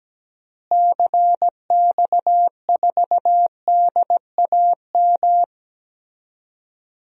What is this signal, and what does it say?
Morse code, heard as a keyed tone, about 17 wpm, 700 Hz. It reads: CX4DAM
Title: DXCC-100 Call Signs - 6 Characters - Repeated 2x - 17wpm